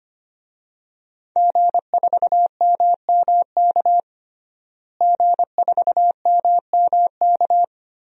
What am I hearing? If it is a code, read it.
Morse code, heard as a keyed tone, about 25 wpm, 700 Hz. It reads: G4MMK G4MMK
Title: DXCC-100 Call Signs - 5 Characters - Repeated 3x - 25wpm